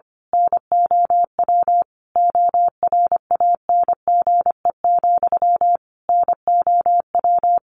Morse code, heard as a keyed tone, about 25 wpm, 700 Hz. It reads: NOW ORANGE, NOW
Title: A Sound Of Thunder 25wpm